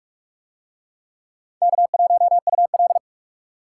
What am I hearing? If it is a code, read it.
Morse code, heard as a keyed tone, about 42 wpm, 700 Hz. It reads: K1UL